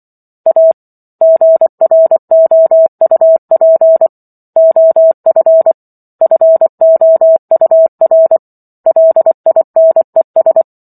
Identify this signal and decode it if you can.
Morse code, heard as a keyed tone, about 24 wpm, 650 Hz. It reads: A GROUP OF FOUR LINES